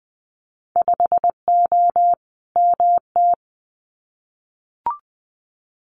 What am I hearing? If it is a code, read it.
Morse code, heard as a keyed tone, about 20 wpm, 700 Hz. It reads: 5O MT
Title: Winter Field Day Exchanges Encore 20wpm